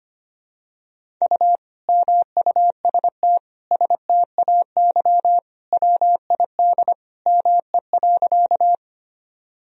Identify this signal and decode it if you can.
Morse code, heard as a keyed tone, about 25 wpm, 700 Hz. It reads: U MUST STAY WID ME.
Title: Sentences from Top 500 Words and Abbreviations 25wpm